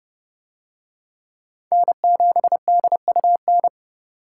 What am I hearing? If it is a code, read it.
Morse code, heard as a keyed tone, about 27 wpm, 700 Hz. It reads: N7DUN